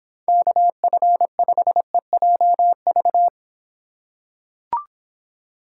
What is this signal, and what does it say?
Morse code, heard as a keyed tone, about 26 wpm, 700 Hz. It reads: KF5EJV